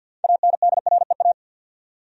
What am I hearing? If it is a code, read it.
Morse code, heard as a keyed tone, about 52 wpm, 700 Hz. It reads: ANDREA
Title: USA Common Names - 50wpm